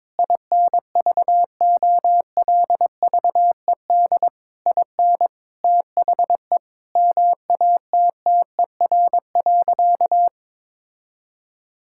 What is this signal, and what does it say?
Morse code, heard as a keyed tone, about 22 wpm, 700 Hz. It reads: INVOLVED IN THE MATTER.
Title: Sentences from Top 1000 Words - Review Part 5 - 22wpm